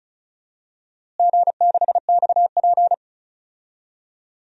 Morse code, heard as a keyed tone, about 36 wpm, 700 Hz. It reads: G6XP
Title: DXCC-100 Call Signs - 4 Characters - Repeated 3x - 35wpm